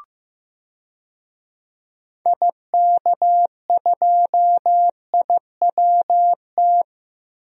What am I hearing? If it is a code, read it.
Morse code, heard as a keyed tone, about 15 wpm, 700 Hz. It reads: IK2IWT
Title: DXCC-100 Call Signs - 6 Characters - Repeated 3x - 15wpm